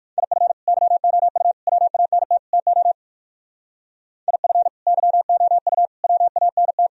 Morse code, heard as a keyed tone, about 59 wpm, 700 Hz. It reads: IF YOU WANT TO IF YOU WANT